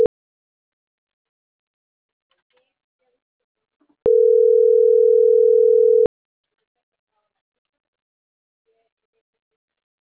Ringback tone (North America), 440 + 480 Hz.